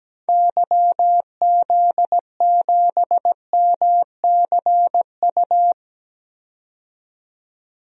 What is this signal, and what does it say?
Morse code, heard as a keyed tone, about 17 wpm, 700 Hz. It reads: YZ7MCU
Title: DXCC-100 Call Signs - 6 Characters - 17wpm